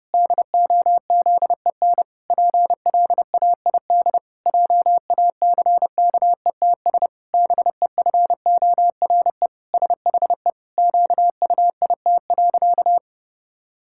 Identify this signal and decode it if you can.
Morse code, heard as a keyed tone, about 32 wpm, 700 Hz. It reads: DOZEN PLAID JACKETS BEFORE SHE QUIT.
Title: Pangram Challenge v2 - 30wpm